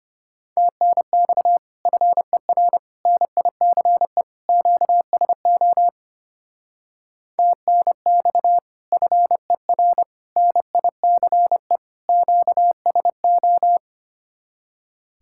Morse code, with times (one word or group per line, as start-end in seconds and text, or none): 0.6-1.6 TNX
1.9-2.8 FER
3.1-4.2 NICE
4.5-5.9 QSO
7.4-8.6 TNX
8.9-10.0 FER
10.4-11.8 NICE
12.1-13.8 QSO